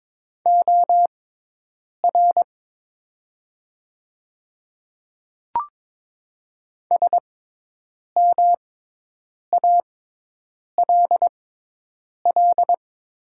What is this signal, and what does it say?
Morse code, heard as a keyed tone, about 22 wpm, spaced as for 8 wpm, 700 Hz. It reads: OR SMALL
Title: 6x Character Spacing - Top 300 Words - 22wpm